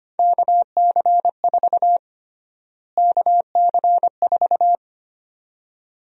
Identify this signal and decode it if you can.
Morse code, heard as a keyed tone, about 25 wpm, 700 Hz. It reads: KC4 KC4
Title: USA Call Sign Prefixes - Repeated 2x - 25wpm